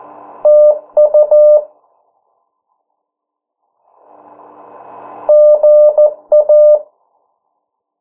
Morse code, heard as a keyed tone, about 14 wpm, 600 Hz. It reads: TU GA
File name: morse mp3 / 09 urok 8